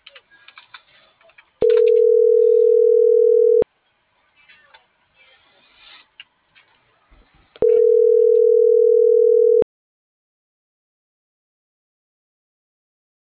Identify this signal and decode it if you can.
Ringback tone (North America), 440 + 480 Hz.